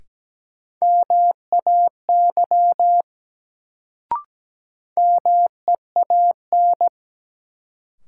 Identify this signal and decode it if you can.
Morse code, heard as a keyed tone, about 17 wpm, 700 Hz. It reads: MAY MEAN
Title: Sets of 2 Words - New Word Drill - 17wpm